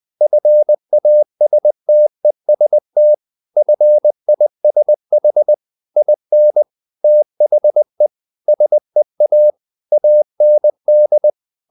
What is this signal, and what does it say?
Morse code, heard as a keyed tone, about 20 wpm, 600 Hz. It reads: FASTEST FISH IN THE SEA AND
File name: the old man and the sea part 2 20wpm